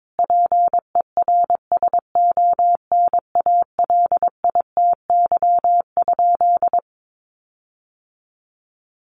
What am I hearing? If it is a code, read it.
Morse code, heard as a keyed tone, about 22 wpm, 700 Hz. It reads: PERSONALITY?
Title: Sentences from Top 1400 Words 22wpm